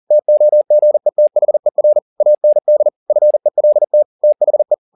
Morse code, heard as a keyed tone, about 40 wpm, 600 Hz. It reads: TOGETHER AND FELT THE
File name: the old man and the sea 2 40wpm